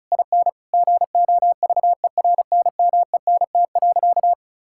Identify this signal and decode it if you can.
Morse code, heard as a keyed tone, about 36 wpm, 700 Hz. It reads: IN GOVERNMENT.
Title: Sentences from Top 1000 Words - Review Part 6 - 35wpm